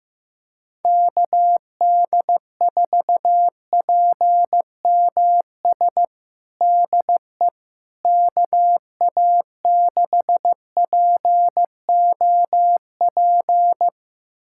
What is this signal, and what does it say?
Morse code, heard as a keyed tone, about 15 wpm, 700 Hz. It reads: KD4PMS DE KA6POP
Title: USA Callsign DE Callsign 15wpm